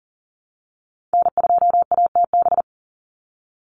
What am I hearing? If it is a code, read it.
Morse code, heard as a keyed tone, about 40 wpm, 700 Hz. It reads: N2ATB
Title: Call Signs - 40wpm